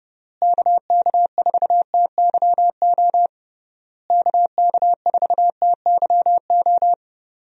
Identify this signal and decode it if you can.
Morse code, heard as a keyed tone, about 31 wpm, 700 Hz. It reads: KK4TYO KK4TYO
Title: USA Call Signs - 6 Characters - Repeated 3x - 30wpm